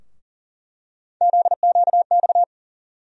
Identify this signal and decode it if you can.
Morse code, heard as a keyed tone, about 40 wpm, 700 Hz. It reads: ZQX
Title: USA Call Sign Postfixes - 3 Letters 40wpm